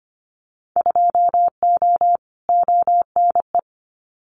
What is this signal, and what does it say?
Morse code, heard as a keyed tone, about 25 wpm, 700 Hz. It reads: 2O ONE
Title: Winter Field Day Exchanges Encore 25wpm